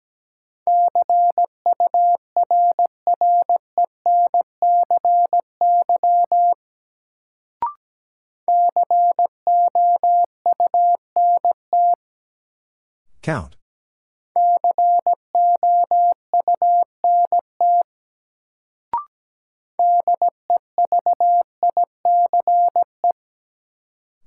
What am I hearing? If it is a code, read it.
Morse code, heard as a keyed tone, about 17 wpm, 700 Hz. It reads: CURRENCY COUNT COUNT DEVICE
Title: Sentences from Top 1500 Words - New Word Drill Alt - 17wpm